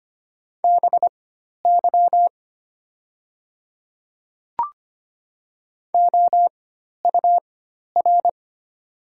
Morse code, heard as a keyed tone, about 25 wpm, spaced as for 12 wpm, 700 Hz. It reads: BY OUR
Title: 4x Character Spacing - Top 200 Words - 25wpm